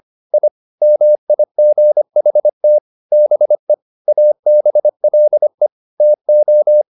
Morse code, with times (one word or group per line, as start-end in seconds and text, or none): 0.3-0.5 I
0.8-2.8 MIGHT
3.1-3.7 BE
4.1-5.7 ABLE
6.0-6.8 TO